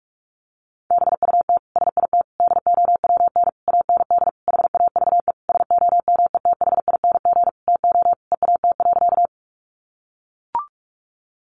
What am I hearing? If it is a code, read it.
Morse code, heard as a keyed tone, about 44 wpm, 700 Hz. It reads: BUT SIT DOWN AND HAVE SOMETHING TO EAT.
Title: Sentences from Top 600 Words 45wpm